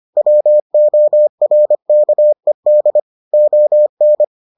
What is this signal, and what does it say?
Morse code, heard as a keyed tone, about 25 wpm, 600 Hz. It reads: WORKED ON